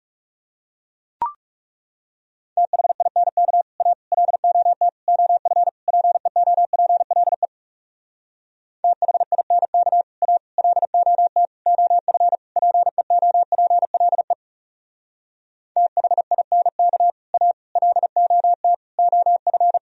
Morse code, heard as a keyed tone, about 40 wpm, 700 Hz. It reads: THINK A LOT OF PEOPLE THINK A LOT OF PEOPLE THINK A LOT OF